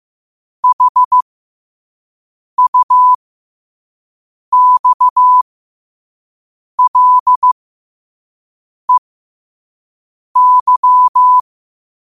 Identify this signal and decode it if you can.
Morse code, heard as a keyed tone, about 15 wpm, spaced as for 5 wpm, 1000 Hz. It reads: HUXLEY
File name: qotdfiles / 5wpm